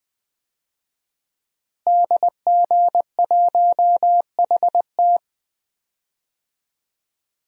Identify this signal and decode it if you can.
Morse code, heard as a keyed tone, about 20 wpm, 700 Hz. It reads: DG1HT
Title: DXCC-100 Call Signs - 5 Characters - Repeated 3x - 20wpm